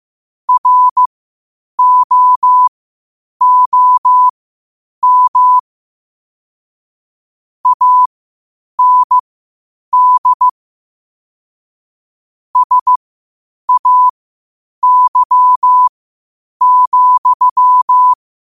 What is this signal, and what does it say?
Morse code, heard as a keyed tone, about 15 wpm, spaced as for 8 wpm, 1000 Hz. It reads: ROOM AND SAY,
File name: qotdfiles / 8wpm